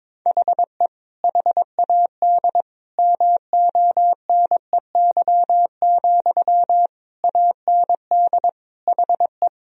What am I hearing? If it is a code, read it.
Morse code, heard as a keyed tone, about 22 wpm, 700 Hz. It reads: HE HAD MONEY, AND HE
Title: Sentences from Top 1000 Words - Review Part 3 - 22wpm